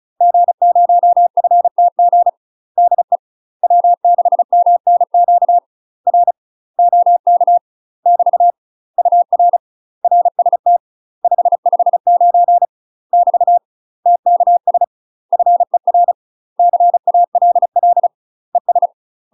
Morse code, heard as a keyed tone, about 36 wpm, 700 Hz. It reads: G0FTG DE W6MNQ R OK <BT> UR RST 559 <BT> TKS FER CALL ES
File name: UK MASTER QSOs TRAINING / 35WPM / UK MASTER QSOs 0035